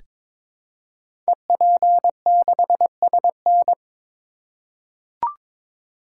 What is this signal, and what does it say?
Morse code, heard as a keyed tone, about 22 wpm, 700 Hz. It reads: EP6SN
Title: Code Groups - 5 Letter:Numbers - 22wpm